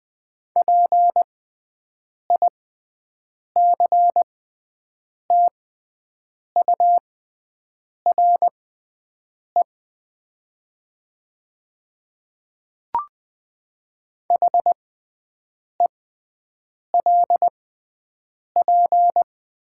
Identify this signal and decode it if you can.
Morse code, heard as a keyed tone, about 20 wpm, spaced as for 7 wpm, 700 Hz. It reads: PICTURE HELP
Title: 6x Character Spacing - Top 300 Words - 20wpm